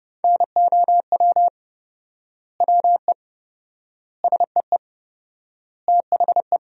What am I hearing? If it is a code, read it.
Morse code, heard as a keyed tone, about 30 wpm, 700 Hz. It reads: NOW WE SEE THE
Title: 4x Word-Spacing - Sentences from Top 200 Words - 30wpm